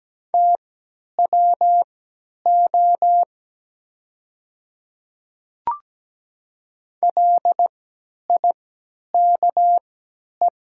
Morse code, heard as a keyed tone, about 17 wpm, spaced as for 10 wpm, 700 Hz. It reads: TWO LIKE
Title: 3x Character Spacing - Top 100 Words - 17wpm